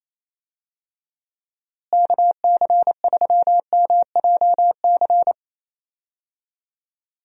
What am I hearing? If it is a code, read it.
Morse code, heard as a keyed tone, about 28 wpm, 700 Hz. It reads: KC3MJC
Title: USA Call Signs - 6 Characters - Repeated 3x - 28wpm